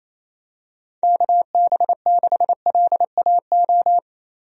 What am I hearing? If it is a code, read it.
Morse code, heard as a keyed tone, about 27 wpm, 700 Hz. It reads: KB6LAO
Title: USA Call Signs - 6 Characters - 28wpm